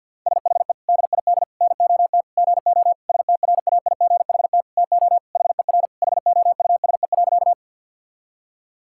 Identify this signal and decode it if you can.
Morse code, heard as a keyed tone, about 50 wpm, 700 Hz. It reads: SHE DID NOT GO STRAIGHT TO HER HOUSE.
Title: Sentences from Top 1000 Words - Review Part 6 - 50wpm